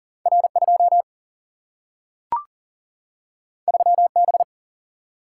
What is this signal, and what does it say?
Morse code, heard as a keyed tone, about 40 wpm, 700 Hz. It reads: R 2 3 B
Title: Single-Letter Single-Number 40wpm